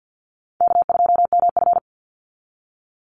Morse code, heard as a keyed tone, about 50 wpm, 700 Hz. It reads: K2MF
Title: Easy Call Signs Repeated 2x - 50wpm